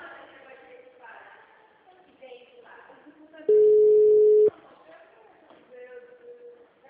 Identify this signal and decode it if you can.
Ringback tone (Europe, 425 Hz), with speech.